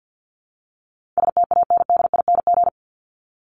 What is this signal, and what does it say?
Morse code, heard as a keyed tone, about 50 wpm, 700 Hz. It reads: STANDING
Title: ING Suffix - Top 100 Words with Base Words 50wpm